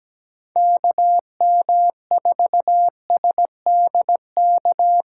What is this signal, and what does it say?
Morse code, heard as a keyed tone, about 17 wpm, 700 Hz. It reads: KM4SDK
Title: Speed Racing - USA 2x3 Call Signs 17wpm